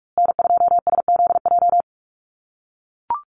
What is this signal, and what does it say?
Morse code, heard as a keyed tone, about 44 wpm, 700 Hz. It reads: N2SZJ
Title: USA Call Signs Repeated 2x 45wpm